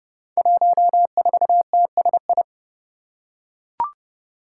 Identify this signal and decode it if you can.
Morse code, heard as a keyed tone, about 31 wpm, 700 Hz. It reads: 14TSI
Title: Five Challenging Letter-Number Combinations 30wpm